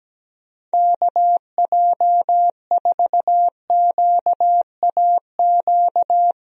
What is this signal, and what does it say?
Morse code, heard as a keyed tone, about 17 wpm, 700 Hz. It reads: KJ4QAQ
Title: USA Call Signs - 6 Characters - Repeated 2x - 17wpm